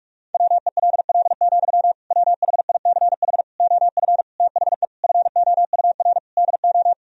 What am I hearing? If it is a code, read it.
Morse code, heard as a keyed tone, about 42 wpm, 700 Hz. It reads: WELL, WHICH OF THE FOUR DO
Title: Sentences from Top 1000 Words - Review Part 1 - 45wpm